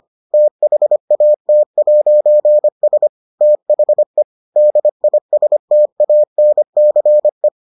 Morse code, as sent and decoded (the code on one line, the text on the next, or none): - .... .- - .----. ... / - .... / -.. .. ... - .- -. -.-. .
THAT'S THE DISTANCE